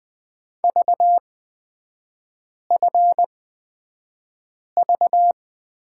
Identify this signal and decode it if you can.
Morse code, heard as a keyed tone, about 20 wpm, spaced as for 5 wpm, 700 Hz. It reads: VFV
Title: 236 Lesson 38e - Letter q - Three Characters 20wpm